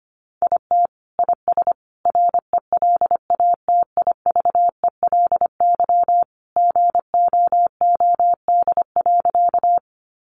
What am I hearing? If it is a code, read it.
Morse code, heard as a keyed tone, about 25 wpm, 700 Hz. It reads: IT IS RELATIVELY GOOD.
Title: Sentences from Top 1100 Words 25wpm